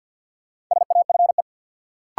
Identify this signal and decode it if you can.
Morse code, heard as a keyed tone, about 51 wpm, 700 Hz. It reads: SAFE
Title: Sentences from Top 900 Words - New Word Drill - 50wpm